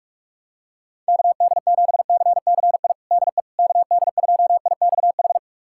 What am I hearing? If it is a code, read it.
Morse code, heard as a keyed tone, about 46 wpm, 690 Hz. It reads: KD7CCI DE KD2IXH
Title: USA Callsign DE Callsign - 6 Characters - 45wpm